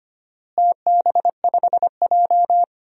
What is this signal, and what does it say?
Morse code, heard as a keyed tone, about 25 wpm, 700 Hz. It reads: TB5J
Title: DXCC-100 Call Signs - 4 Characters - Repeated 2x - 25wpm